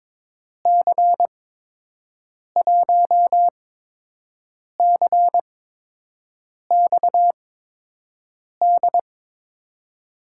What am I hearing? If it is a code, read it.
Morse code, heard as a keyed tone, about 22 wpm, spaced as for 6 wpm, 700 Hz. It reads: C1CXD
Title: USA Call Signs - 6 Characters - 8x Character Spacing - 22wpm